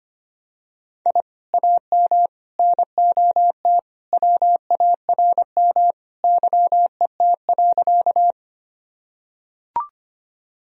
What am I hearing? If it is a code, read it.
Morse code, heard as a keyed tone, about 25 wpm, 700 Hz. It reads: I AM NOT WARM YET.
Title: Sentences from Top 1000 Words - Review Part 2 - 25wpm